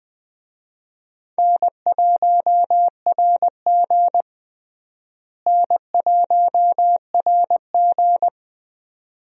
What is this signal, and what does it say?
Morse code, heard as a keyed tone, about 20 wpm, 700 Hz. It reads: N1RG N1RG
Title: USA Call Signs - 4 Characters - Repeated 2x - 20wpm